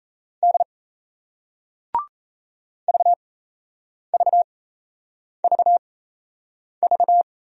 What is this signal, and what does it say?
Morse code, heard as a keyed tone, about 38 wpm, spaced as for 8 wpm, 700 Hz. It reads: D VVVV